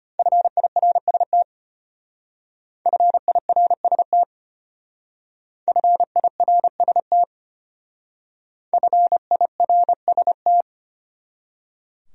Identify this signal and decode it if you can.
Morse code, heard as a keyed tone, about 32 wpm, 700 Hz. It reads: FIRST FIRST FIRST FIRST